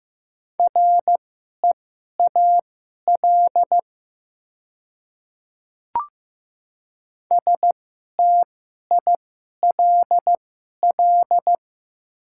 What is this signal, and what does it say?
Morse code, heard as a keyed tone, about 15 wpm, spaced as for 11 wpm, 700 Hz. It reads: REAL STILL
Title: 2x Character Spacing - Top 300 Words - 15wpm